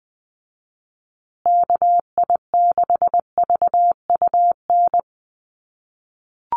Morse code, heard as a keyed tone, about 20 wpm, 700 Hz. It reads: KI6VUN
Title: USA Call Signs Repeated 2x 20wpm